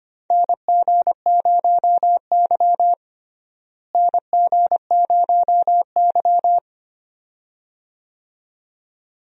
Morse code, heard as a keyed tone, about 25 wpm, 700 Hz. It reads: NG0Y NG0Y
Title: USA Call Signs - 4 Characters - Repeated 2x - 25wpm